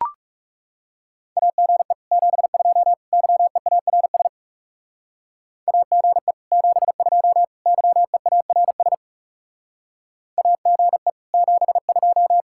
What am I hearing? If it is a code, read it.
Morse code, heard as a keyed tone, about 37 wpm, 700 Hz. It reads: AGE 72 YEARS AGE 72 YEARS AGE 72
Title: Speed-Racing Common QSO Phrases Encore - 30wpm